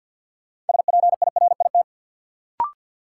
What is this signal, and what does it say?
Morse code, heard as a keyed tone, about 51 wpm, 700 Hz. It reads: SPIRIT
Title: Sentences from Top 1400 Words - New Word Drill - 50wpm